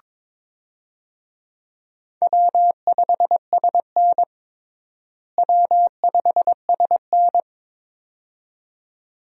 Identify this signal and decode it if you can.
Morse code, heard as a keyed tone, about 22 wpm, 700 Hz. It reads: W5SN W5SN
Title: USA Call Signs - 4 Characters - Repeated 2x - 22wpm